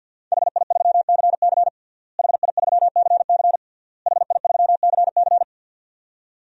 Morse code, heard as a keyed tone, about 52 wpm, 700 Hz. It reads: HI3CC HI3CC HI3CC
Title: DXCC-100 Call Signs - 5 Characters - Repeated 3x - 50wpm